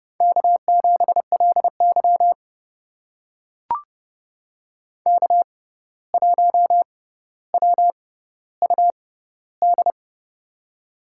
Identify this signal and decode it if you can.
Morse code, heard as a keyed tone, about 30 wpm, spaced as for 14 wpm, 700 Hz. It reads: K7LY K1WUD
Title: USA Call Signs - 6x Character Spacing - 30wpm